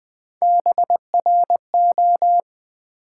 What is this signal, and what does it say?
Morse code, heard as a keyed tone, about 21 wpm, 700 Hz. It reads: BRO
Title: Build A Word - 3 Letters At A Time - 9-Letter Words - 20wpm